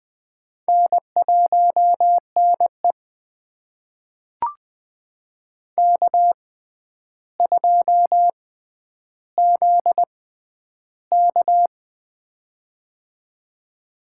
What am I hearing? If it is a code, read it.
Morse code, heard as a keyed tone, about 20 wpm, spaced as for 10 wpm, 700 Hz. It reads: N1NE K2ZK
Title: USA Call Signs - 4 Characters - 6x Character Spacing - 20wpm